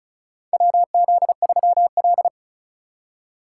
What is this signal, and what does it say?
Morse code, heard as a keyed tone, about 36 wpm, 700 Hz. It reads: WZ3L
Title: USA Call Signs - 4 Characters - 2x Character Spacing - 35wpm